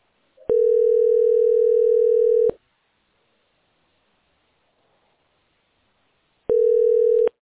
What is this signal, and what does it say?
Ringback tone (North America), 440 + 480 Hz.